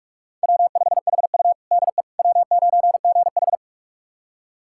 Morse code, heard as a keyed tone, about 46 wpm, 700 Hz. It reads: W5HU DE W9GH
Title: USA Callsign DE Callsign - 4 Characters - 45wpm